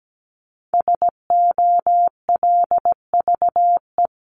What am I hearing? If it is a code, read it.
Morse code, heard as a keyed tone, about 17 wpm, 700 Hz. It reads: SOLVE
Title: Top 751-1000 Words 17wpm